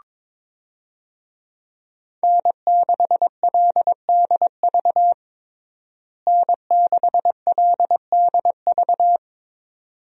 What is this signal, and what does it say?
Morse code, heard as a keyed tone, about 21 wpm, 700 Hz. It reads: N6LDV N6LDV
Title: USA Call Signs - 5 Characters - Repeated 2x - 22wpm